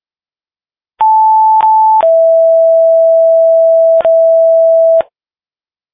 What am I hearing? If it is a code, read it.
Two-tone page, heard as a one-second tone, then a three-second tone, about 880 then 660 Hz. It is heard from 1.0 to 5.0 s.